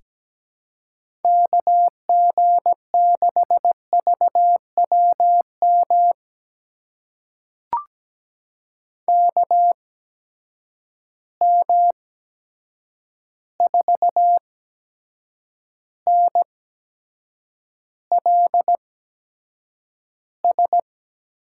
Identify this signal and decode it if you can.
Morse code, heard as a keyed tone, about 17 wpm, spaced as for 7 wpm, 700 Hz. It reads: KG6VWM KM4NLS
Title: USA Call Signs - 6 Characters - 8x Character Spacing - 17wpm